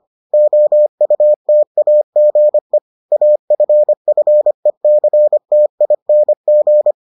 Morse code, heard as a keyed tone, about 25 wpm, 600 Hz. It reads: OUTAGE AFFECTING